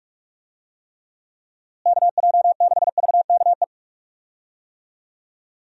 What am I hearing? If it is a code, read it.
Morse code, heard as a keyed tone, about 42 wpm, 700 Hz. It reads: KJ6VKE
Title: USA Call Signs - 6 Characters - Repeated 3x - 45wpm